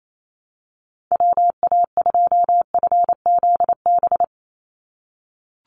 Morse code, heard as a keyed tone, about 28 wpm, 700 Hz. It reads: WA2FZB
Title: Call Signs - 28wpm